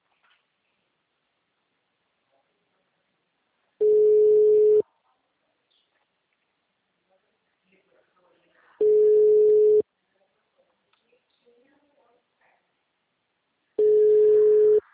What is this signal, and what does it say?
Ringback tone (Europe, 425 Hz), with speech.